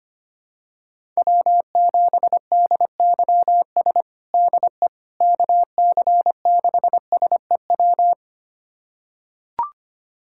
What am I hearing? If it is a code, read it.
Morse code, heard as a keyed tone, about 25 wpm, 700 Hz. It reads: W7DYS DE KC6SEW
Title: USA Callsign DE Callsign 25wpm